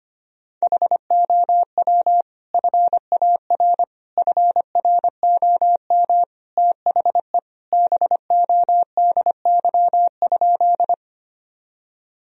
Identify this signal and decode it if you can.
Morse code, heard as a keyed tone, about 25 wpm, 700 Hz. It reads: HOW FAR FROM THE BODY?
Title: Sentences from Top 1000 Words - Review Part 2 - 25wpm